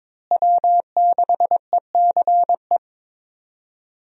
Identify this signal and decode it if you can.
Morse code, heard as a keyed tone, about 22 wpm, 700 Hz. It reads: W6ECE
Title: USA Call Signs - 5 Characters - Repeated 2x - 22wpm